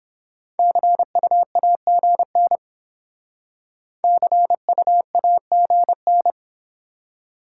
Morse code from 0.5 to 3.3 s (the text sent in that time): CUAGN